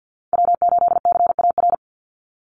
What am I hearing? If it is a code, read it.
Morse code, heard as a keyed tone, about 50 wpm, 700 Hz. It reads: W8CAR
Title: Call Signs - 50wpm